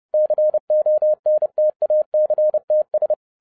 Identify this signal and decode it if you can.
Morse code, heard as a keyed tone, about 30 wpm, 610 Hz. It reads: CONTACTS